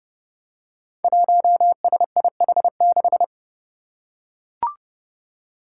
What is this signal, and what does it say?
Morse code, heard as a keyed tone, about 32 wpm, 700 Hz. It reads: 1SIH6
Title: Five Challenging Letter-Number Combinations 30wpm